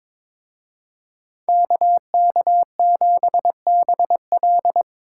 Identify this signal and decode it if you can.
Morse code, heard as a keyed tone, about 25 wpm, 700 Hz. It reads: KK7BL
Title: Speed Racing - USA 2x2 Call Signs 22wpm